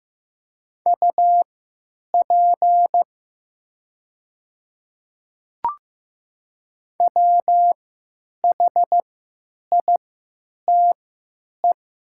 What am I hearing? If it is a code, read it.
Morse code, heard as a keyed tone, about 15 wpm, spaced as for 9 wpm, 700 Hz. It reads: UP WHITE
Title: 3x Character Spacing - Top 300 Words - 15wpm